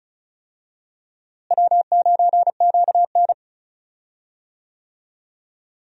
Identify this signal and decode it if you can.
Morse code, heard as a keyed tone, about 34 wpm, 700 Hz. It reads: W9QN
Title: USA Call Signs - 4 Characters - Repeated 2x - 35wpm